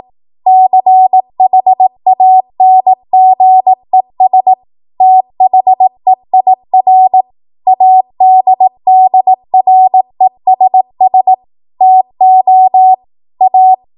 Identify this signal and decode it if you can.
Morse code, heard as a keyed tone, about 18 wpm, 750 Hz. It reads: CHANGES THEIR ADDRESS TO A
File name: Morse / 18WPM